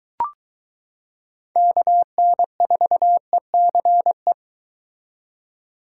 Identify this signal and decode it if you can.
Morse code, heard as a keyed tone, about 22 wpm, 700 Hz. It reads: KN4ECE